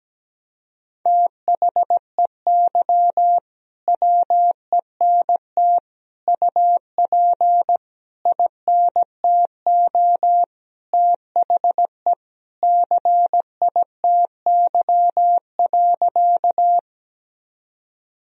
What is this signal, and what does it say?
Morse code, heard as a keyed tone, about 17 wpm, 700 Hz. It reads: THEY WENT UP INTO THE CITY.
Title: Sentences from Top 1000 Words - Review Part 1 - 17wpm